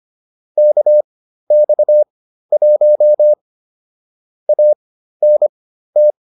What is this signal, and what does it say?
Morse code, heard as a keyed tone, about 25 wpm, spaced as for 13 wpm, 600 Hz. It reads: KX1 ANT
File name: QSO 207 13